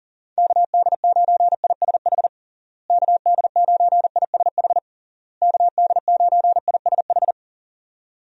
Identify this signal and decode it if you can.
Morse code, heard as a keyed tone, about 40 wpm, 700 Hz. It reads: KD9ISH KD9ISH KD9ISH
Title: USA Call Signs - 6 Characters - Repeated 3x - 40wpm